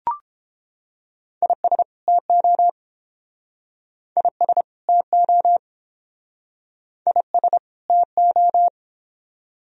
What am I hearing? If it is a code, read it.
Morse code, heard as a keyed tone, about 29 wpm, 700 Hz. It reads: IS TO IS TO IS TO